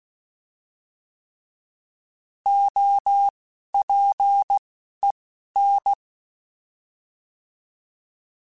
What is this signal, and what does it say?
Morse code, heard as a keyed tone, about 16 wpm, spaced as for 11 wpm, 800 Hz. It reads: OPEN
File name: Alphabetical / Alphabetical mp3s / Os @10